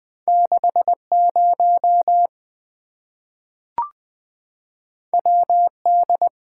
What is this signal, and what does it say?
Morse code, heard as a keyed tone, about 20 wpm, 700 Hz. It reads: 6 0 W D
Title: Two Letter-Number Combinations - 20wpm